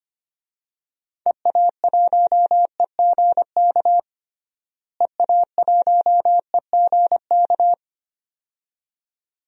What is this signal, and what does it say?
Morse code, heard as a keyed tone, about 25 wpm, 700 Hz. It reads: EA1EGK EA1EGK
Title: DXCC-100 Call Signs - 6 Characters - Repeated 2x - 25wpm